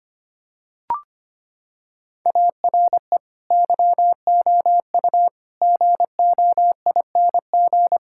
Morse code, heard as a keyed tone, about 25 wpm, 700 Hz. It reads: ARE YOU GOING
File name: practice / 25wpm / Sentences with 4-Word Phrases - 25wpm